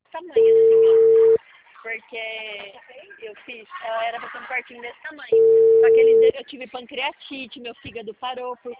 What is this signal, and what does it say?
Ringback tone (China, 450 Hz), with speech.